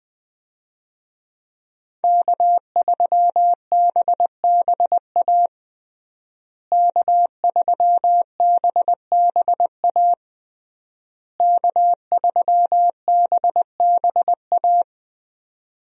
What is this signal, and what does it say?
Morse code, heard as a keyed tone, about 20 wpm, 700 Hz. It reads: K3BBA K3BBA K3BBA
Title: USA Call Signs - 5 Characters - Repeated 3x - 20wpm